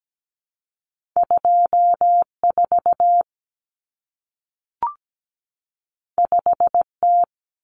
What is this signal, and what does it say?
Morse code, heard as a keyed tone, about 17 wpm, 700 Hz. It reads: 24 5T
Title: Common Cut Numbers - 2-Digit 17wpm